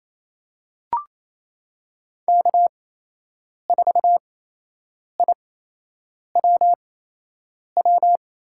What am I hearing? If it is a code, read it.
Morse code, heard as a keyed tone, about 28 wpm, spaced as for 8 wpm, 700 Hz. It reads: K4IWW
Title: USA Call Signs - 5 Characters - 8x Character Spacing - 28wpm